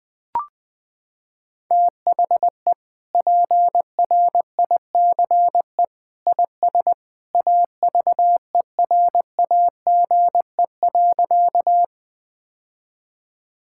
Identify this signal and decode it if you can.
Morse code, heard as a keyed tone, about 20 wpm, 700 Hz. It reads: THE PRICE IS AVERAGE.
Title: Sentences from Top 1000 Words - Review Part 6 - 20wpm